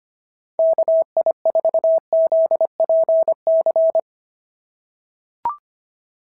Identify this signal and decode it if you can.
Morse code, heard as a keyed tone, about 25 wpm, 650 Hz. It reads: KI4ZPC
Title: SOTA QSO Practice 25wpm